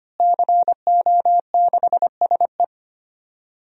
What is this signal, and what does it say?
Morse code, heard as a keyed tone, about 25 wpm, 700 Hz. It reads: CO6SE
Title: DXCC-100 Call Signs - 5 Characters - Repeated 2x - 25wpm